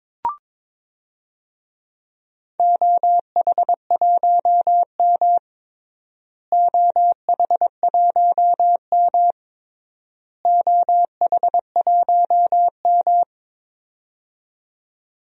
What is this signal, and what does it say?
Morse code, heard as a keyed tone, about 22 wpm, 700 Hz. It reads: OH1M OH1M OH1M
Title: DXCC-100 Call Signs - 4 Characters - Repeated 3x - 22wpm